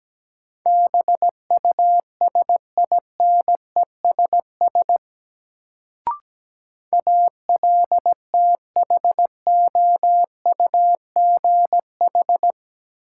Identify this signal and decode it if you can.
Morse code, heard as a keyed tone, about 17 wpm, 700 Hz. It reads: BUSINESS ALTHOUGH
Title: Sentences from Top 600 Words - New Word Drill - 17wpm